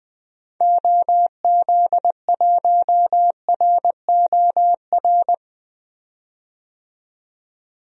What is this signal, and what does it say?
Morse code, heard as a keyed tone, about 21 wpm, 700 Hz. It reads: OZ1ROR
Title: DXCC-100 Call Signs - 6 Characters - 20wpm